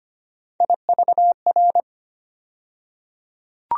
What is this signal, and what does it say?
Morse code, heard as a keyed tone, about 25 wpm, 700 Hz. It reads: IVR
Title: USA Call Sign Postfixes - 3 Letters 25wpm